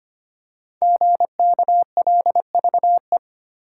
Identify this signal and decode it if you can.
Morse code, heard as a keyed tone, about 25 wpm, 700 Hz. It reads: GKLVE
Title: Code Groups - 5 Letter:Numbers - 25wpm